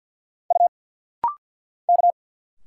Morse code, heard as a keyed tone, about 50 wpm, 700 Hz. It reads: U K
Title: Single Letters - Rapid-Fire 50wpm